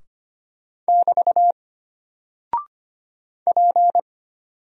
Morse code, heard as a keyed tone, about 25 wpm, 700 Hz. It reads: <BT> P
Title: Single Character (Letter:Number:Punctuation:Prosign) Alt 25wpm